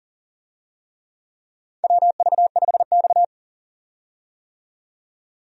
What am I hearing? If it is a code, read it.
Morse code, heard as a keyed tone, about 38 wpm, 700 Hz. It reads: WV5X
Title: USA Call Signs - 4 Characters - 40wpm